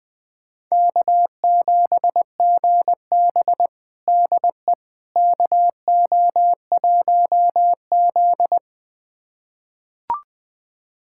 Morse code, heard as a keyed tone, about 20 wpm, 700 Hz. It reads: K7GB DE KO1Z
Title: USA Callsign DE Callsign - 4 Characters - 20wpm